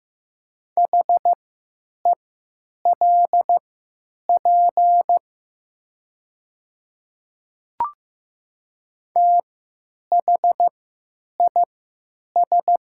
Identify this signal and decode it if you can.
Morse code, heard as a keyed tone, about 15 wpm, spaced as for 9 wpm, 700 Hz. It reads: HELP THIS
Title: 3x Character Spacing - Top 500 Words - 15wpm